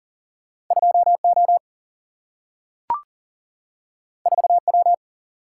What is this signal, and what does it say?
Morse code, heard as a keyed tone, about 38 wpm, 700 Hz. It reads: 2 O 4 W
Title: Single-Letter Single-Number 40wpm